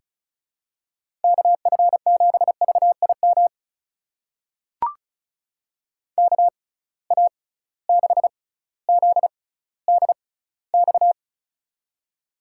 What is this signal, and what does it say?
Morse code, heard as a keyed tone, about 34 wpm, spaced as for 18 wpm, 700 Hz. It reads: KF7VIM KA6ZDX
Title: USA Call Signs - 6 Characters - 6x Character Spacing - 35wpm